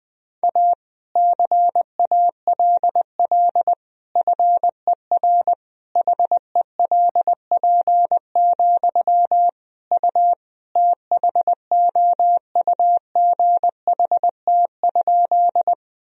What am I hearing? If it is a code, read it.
Morse code, heard as a keyed tone, about 20 wpm, 700 Hz. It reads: A CALL FER HELP, U THOUGHT?
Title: Sentences from Top 300 Words and Abbreviations 20wpm